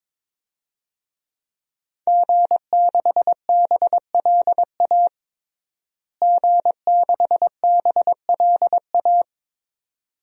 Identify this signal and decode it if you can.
Morse code, heard as a keyed tone, about 22 wpm, 700 Hz. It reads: G6BLA G6BLA
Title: DXCC-100 Call Signs - 5 Characters - Repeated 2x - 22wpm